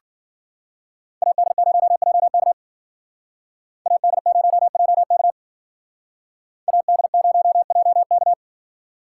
Morse code, heard as a keyed tone, about 49 wpm, 700 Hz. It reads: AD0JK AD0JK AD0JK
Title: Speed Racing - USA 2x2 Call Signs 40wpm